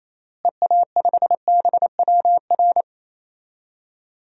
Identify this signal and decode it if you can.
Morse code, heard as a keyed tone, about 28 wpm, 700 Hz. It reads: EA5BWR
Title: DXCC-100 Call Signs - 6 Characters - Repeated 3x - 28wpm